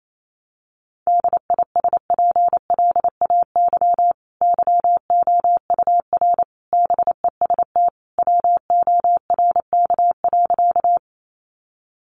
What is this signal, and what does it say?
Morse code, heard as a keyed tone, about 28 wpm, 700 Hz. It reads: DISPLAY YOUR BEST WORK.
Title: Sentences from Top 1200 Words 28wpm